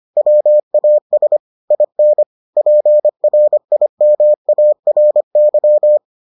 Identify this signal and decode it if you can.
Morse code, heard as a keyed tone, about 25 wpm, 600 Hz. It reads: WAS IN PRIMARY